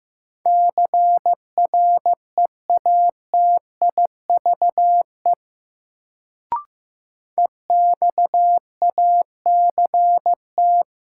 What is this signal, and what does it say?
Morse code, heard as a keyed tone, about 15 wpm, 700 Hz. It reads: CREATIVE EXACT
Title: Sentences from Top 1300 Words - New Word Drill Alt - 15wpm